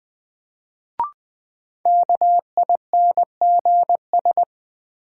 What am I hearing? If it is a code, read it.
Morse code, heard as a keyed tone, about 20 wpm, 700 Hz. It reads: KINGS
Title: Sentences from Top 1000 Words - Review Part 6 - New Word Drill - 20wpm